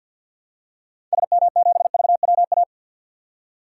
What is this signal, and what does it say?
Morse code, heard as a keyed tone, about 49 wpm, 700 Hz. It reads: SM7VWA
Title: DXCC-100 Call Signs - 6 Characters - 50wpm